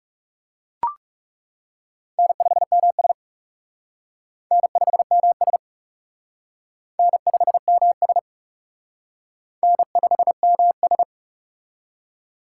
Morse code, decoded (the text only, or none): N5MS N5MS N5MS N5MS